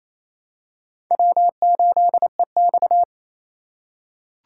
Morse code, heard as a keyed tone, about 28 wpm, 700 Hz. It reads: W8EX
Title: USA Call Signs - 4 Characters - 3x Character Spacing - 28wpm